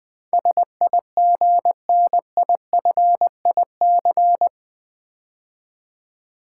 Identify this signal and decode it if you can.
Morse code, heard as a keyed tone, about 20 wpm, 700 Hz. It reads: SIGNIFIC
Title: Build A Word - 4 Letters At A Time - 12-Letter Words - 20wpm